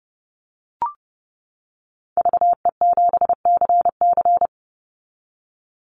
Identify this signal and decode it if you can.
Morse code, heard as a keyed tone, about 30 wpm, 700 Hz. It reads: VE7CC
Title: Call Signs - 30wpm